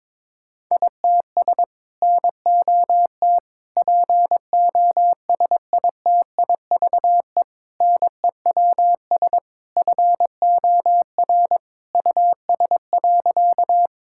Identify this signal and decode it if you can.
Morse code, heard as a keyed tone, about 22 wpm, 700 Hz. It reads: ITS NOT POSITIVE NEWS FOR US.
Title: Sentences from Top 1000 Words - Review Part 5 - 22wpm